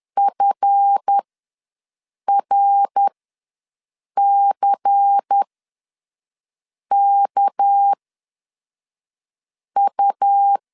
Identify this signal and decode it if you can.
Morse code, heard as a keyed tone, about 10 wpm, spaced as for 5 wpm, 800 Hz. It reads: FRCKU
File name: LEZIONI / 7 GF9Z / 7 GF ws 10wpm 3918